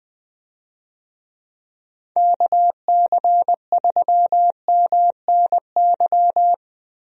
Morse code, heard as a keyed tone, about 21 wpm, 700 Hz. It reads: KC3MNY
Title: USA Call Signs - 6 Characters - Repeated 2x - 20wpm